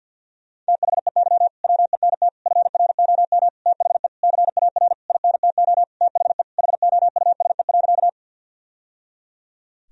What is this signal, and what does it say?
Morse code, heard as a keyed tone, about 50 wpm, 700 Hz. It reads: THEY WENT FROM THE CAR INTO THE HOUSE.
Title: Sentences from Top 1000 Words - Review Part 1 - 50wpm